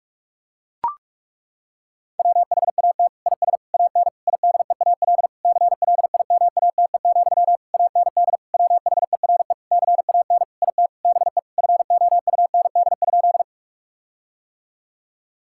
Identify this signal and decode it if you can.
Morse code, heard as a keyed tone, about 41 wpm, 700 Hz. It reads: WHAT IS AN IDEAL CLIMATE, AND WHERE CAN IT BE FOUND?
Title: Sentences from Top 1600 Words 45wpm